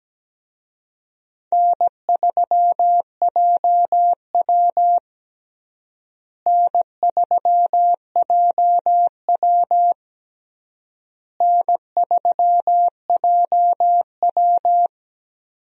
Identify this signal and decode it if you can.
Morse code, heard as a keyed tone, about 17 wpm, 700 Hz. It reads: N3JW N3JW N3JW
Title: USA Call Signs - 4 Characters - Repeated 3x - 17wpm